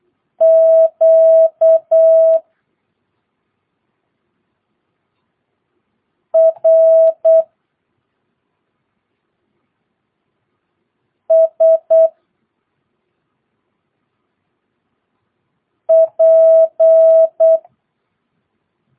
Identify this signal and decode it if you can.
Morse code, heard as a keyed tone, about 8 wpm, 650 Hz. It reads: Q R S P
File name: jh8cbh / 2022cw